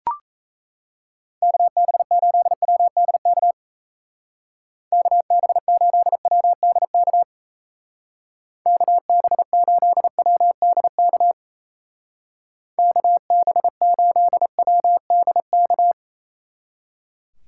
Morse code, heard as a keyed tone, about 35 wpm, 700 Hz. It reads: KB8WDK KB8WDK KB8WDK KB8WDK
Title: Speed Racing - USA 2x3 Call Signs 28wpm